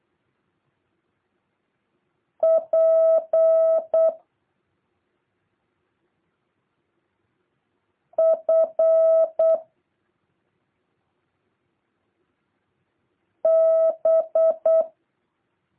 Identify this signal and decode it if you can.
Morse code, heard as a keyed tone, about 8 wpm, 650 Hz. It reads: P F B
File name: jh8cbh / 2023cw